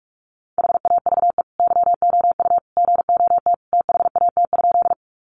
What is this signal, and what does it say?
Morse code, heard as a keyed tone, about 44 wpm, 700 Hz. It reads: HAVE YOU GOT THAT?
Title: Sentences from Top 300 Words 45wpm